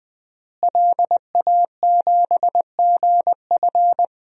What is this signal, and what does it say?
Morse code, heard as a keyed tone, about 20 wpm, 700 Hz. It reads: LA7GF
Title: DXCC-100 Call Signs - 5 Characters - Repeated 3x - 20wpm